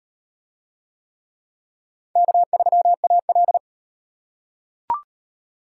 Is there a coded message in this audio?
Yes: Morse code from 2.2 to 3.6 s, heard as a keyed tone, about 37 wpm, 700 Hz. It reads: K3AL